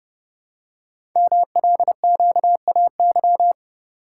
Morse code, heard as a keyed tone, about 30 wpm, 700 Hz. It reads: MLQAY
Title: Code Groups - 5 Letters - 30wpm